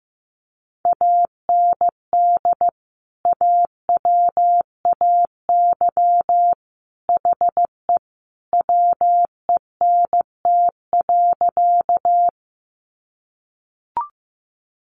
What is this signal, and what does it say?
Morse code, heard as a keyed tone, about 15 wpm, 700 Hz. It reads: AND AWAY HE WENT.
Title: Sentences from Top 600 Words 15wpm